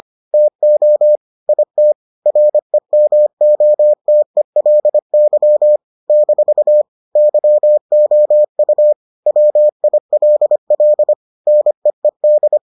Morse code, heard as a keyed tone, about 25 wpm, 600 Hz. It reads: TO IT REMOTELY - YOU WILL NEED